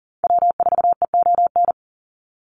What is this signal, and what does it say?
Morse code, heard as a keyed tone, about 40 wpm, 700 Hz. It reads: W4EON
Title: NA SOTA Activator Callsigns 40wpm